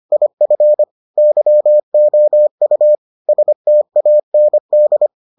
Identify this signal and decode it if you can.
Morse code, heard as a keyed tone, about 25 wpm, 600 Hz. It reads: IF YOU STAND